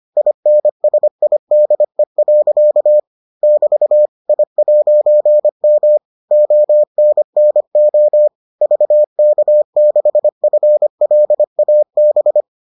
Morse code, heard as a keyed tone, about 25 wpm, 600 Hz. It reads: INSIDE. <BT> I'M ONNO VK6FLAB